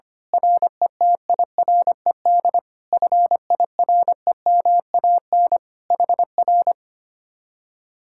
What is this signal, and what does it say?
Morse code, heard as a keyed tone, about 24 wpm, 700 Hz. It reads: RETIRED FIREMAN HR
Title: Speed-Racing Common QSO Phrases - 25wpm